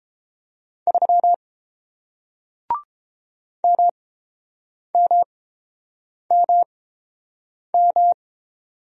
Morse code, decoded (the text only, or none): 3 MMMM